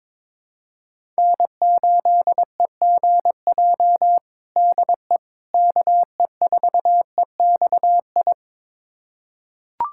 Morse code, heard as a keyed tone, about 22 wpm, 700 Hz. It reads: N8EGJ DE KE4EXI
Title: USA Callsign DE Callsign 22wpm